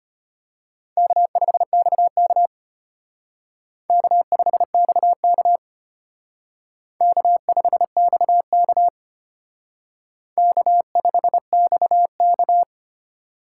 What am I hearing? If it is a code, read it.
Morse code, heard as a keyed tone, about 29 wpm, 700 Hz. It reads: K5XK K5XK K5XK K5XK
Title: Speed Racing - USA 1x2 Call Signs 25wpm